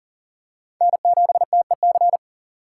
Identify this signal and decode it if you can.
Morse code, heard as a keyed tone, about 40 wpm, 700 Hz. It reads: N7TEC
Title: USA Call Signs - 6x Character Spacing - 40wpm